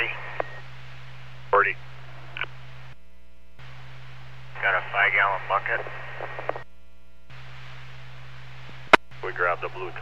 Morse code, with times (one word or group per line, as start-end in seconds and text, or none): none